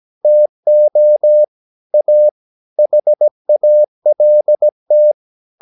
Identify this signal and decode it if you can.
Morse code, heard as a keyed tone, about 17 wpm, 600 Hz. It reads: TO A HALT
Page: Romanen Never af Ken Follet på engelsk.